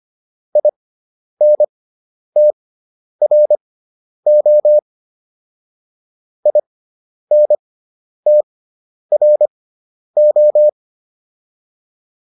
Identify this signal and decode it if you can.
Morse code, heard as a keyed tone, about 25 wpm, spaced as for 10 wpm, 600 Hz. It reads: INTRO INTRO